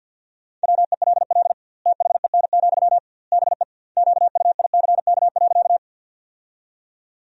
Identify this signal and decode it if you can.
Morse code, heard as a keyed tone, about 49 wpm, 700 Hz. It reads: WELL THEN, BE QUICK.
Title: 2x Word-Spacing - Sentences from Top 500 Words - 50wpm